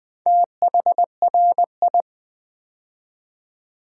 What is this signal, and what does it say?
Morse code, heard as a keyed tone, about 20 wpm, 700 Hz. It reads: THRI
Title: Build A Word - 2 Letters At A Time - 8-Letter Words - 20wpm